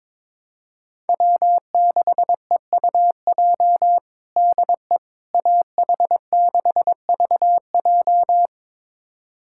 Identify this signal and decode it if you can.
Morse code, heard as a keyed tone, about 22 wpm, 700 Hz. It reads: W6EUJ DE AH6VJ
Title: USA Callsign DE Callsign - 5 Characters - 22wpm